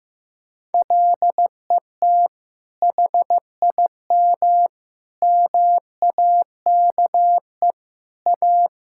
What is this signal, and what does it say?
Morse code, heard as a keyed tone, about 15 wpm, 700 Hz. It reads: LET HIM MAKE A
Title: Sentences from Top 1000 Words - Review Part 6 - 15wpm